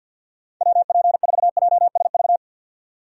Morse code, heard as a keyed tone, about 51 wpm, 700 Hz. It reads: WP4JSV
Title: USA Call Signs - 6 Characters - 50wpm